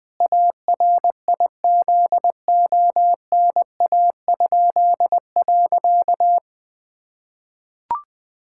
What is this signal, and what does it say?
Morse code, heard as a keyed tone, about 20 wpm, 700 Hz. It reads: ARIZONA?.
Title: Pangram Challenge v2 - 20wpm